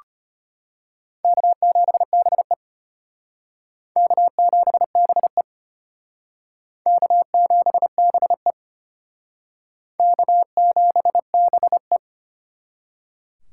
Morse code, heard as a keyed tone, about 32 wpm, 700 Hz. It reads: K7BE K7BE K7BE K7BE